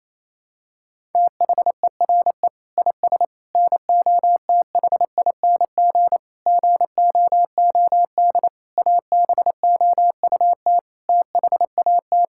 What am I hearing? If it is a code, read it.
Morse code, heard as a keyed tone, about 28 wpm, 700 Hz. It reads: THERE IS NOTHING GOOD ABOUT THAT
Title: Sentences from Top 1000 Words - Review Part 5 - 28wpm